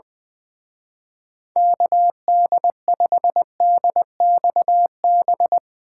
Morse code, heard as a keyed tone, about 20 wpm, 700 Hz. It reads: KD5DXB
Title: Speed Racing - USA 2x3 Call Signs 20wpm